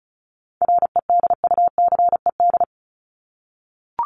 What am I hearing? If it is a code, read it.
Morse code, heard as a keyed tone, about 34 wpm, 700 Hz. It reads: REDUCED
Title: ED Suffix - Top 100 Words with Base Words 35wpm